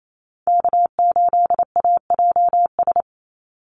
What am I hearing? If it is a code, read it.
Morse code, heard as a keyed tone, about 28 wpm, 700 Hz. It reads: K8AJS
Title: Call Signs - 28wpm